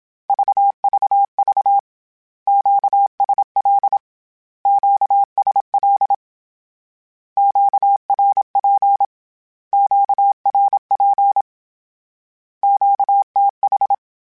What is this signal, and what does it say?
Morse code, heard as a keyed tone, about 27 wpm, 800 Hz. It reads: VVV QSL QSL QRP QRP QTH
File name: Q Calls / Q Calls-mp3s / Q Calls Twice @25